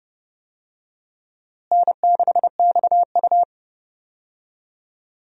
Morse code, heard as a keyed tone, about 32 wpm, 700 Hz. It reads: N6XU
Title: USA Call Signs - 4 Characters - Repeated 3x - 30wpm